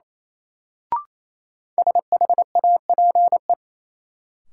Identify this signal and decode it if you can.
Morse code, heard as a keyed tone, about 28 wpm, 700 Hz. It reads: SHAPE
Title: Sets of 5 Words Encore - New Word Drill - 28wpm